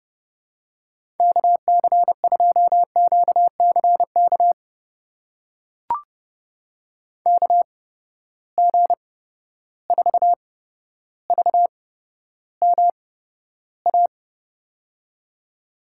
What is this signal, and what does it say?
Morse code, heard as a keyed tone, about 30 wpm, spaced as for 13 wpm, 700 Hz. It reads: KC2QCK KG4VMA
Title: USA Call Signs - 6 Characters - 8x Character Spacing - 30wpm